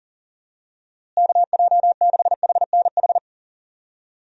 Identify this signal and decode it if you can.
Morse code, heard as a keyed tone, about 40 wpm, 700 Hz. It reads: KJ6HNH
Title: USA Call Signs - 6 Characters - Repeated 3x - 40wpm